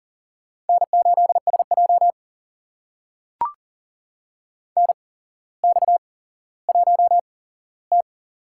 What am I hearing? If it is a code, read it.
Morse code, heard as a keyed tone, about 42 wpm, spaced as for 17 wpm, 700 Hz. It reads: N8SJ NX1T
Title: USA Call Signs - 4 Characters - 8x Character Spacing - 40wpm